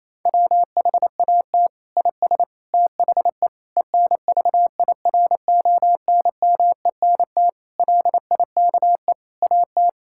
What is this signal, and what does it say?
Morse code, heard as a keyed tone, about 28 wpm, 700 Hz. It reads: WHAT IS THE ENVIRONMENT LIKE AT
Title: Sentences from Top 1000 Words - Review Part 5 - 28wpm